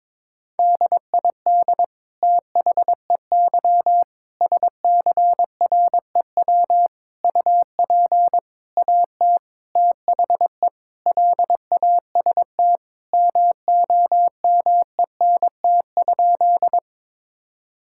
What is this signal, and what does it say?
Morse code, heard as a keyed tone, about 22 wpm, 700 Hz. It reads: DID THEY SCREW UP AT THE LAST MOMENT?
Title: Sentences from Top 1600 Words 22wpm